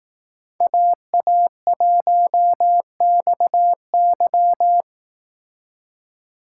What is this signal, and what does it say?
Morse code, heard as a keyed tone, about 18 wpm, 700 Hz. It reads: AA1XY
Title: Speed Racing - USA 2x2 Call Signs 15wpm